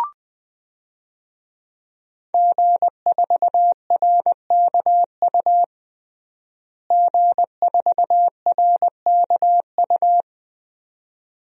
Morse code, heard as a keyed tone, about 20 wpm, 700 Hz. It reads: G4RKU G4RKU
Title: DXCC-100 Call Signs - 5 Characters - Repeated 2x - 20wpm